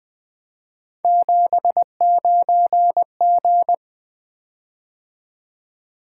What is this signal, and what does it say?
Morse code, heard as a keyed tone, about 21 wpm, 700 Hz. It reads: 79G
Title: Three Letter-Number Combinations - 20wpm